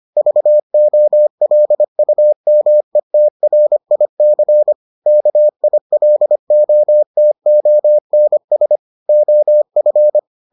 Morse code, heard as a keyed tone, about 25 wpm, 600 Hz. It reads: VOLUMETRIC KILOTONS OF